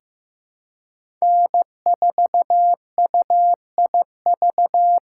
Morse code, heard as a keyed tone, about 15 wpm, 700 Hz. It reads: N4UIV
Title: USA Call Signs - 5 Characters - 4x Character Spacing - 15wpm